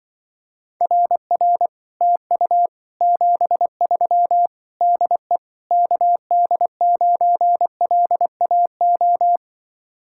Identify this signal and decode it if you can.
Morse code, heard as a keyed tone, about 24 wpm, 700 Hz. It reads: RR TU 73 DE KD9LAO
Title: Speed-Racing Common QSO Phrases - 20wpm